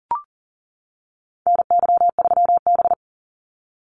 Morse code, heard as a keyed tone, about 40 wpm, 700 Hz. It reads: NY3B
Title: Call Signs - 40wpm